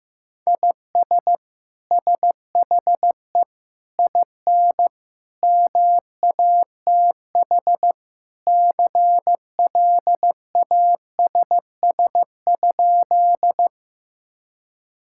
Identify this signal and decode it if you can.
Morse code, heard as a keyed tone, about 15 wpm, 700 Hz. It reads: IS SHE IN MATH CLASS?
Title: Sentences from Top 1000 Words - Review Part 6 - 15wpm